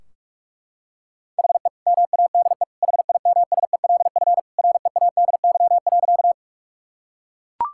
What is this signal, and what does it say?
Morse code, heard as a keyed tone, about 42 wpm, 700 Hz. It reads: HE MADE HIMSELF READY.
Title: Sentences from Top 1600 Words 45wpm